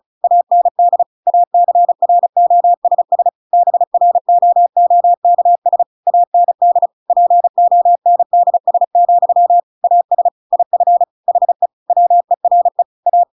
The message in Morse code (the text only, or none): AND ACROSS BROOKS AND PONDS, AS IF HE WERE A